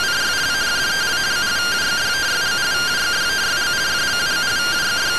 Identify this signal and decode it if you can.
Two-tone teleprinter signal, about 170 Hz shift.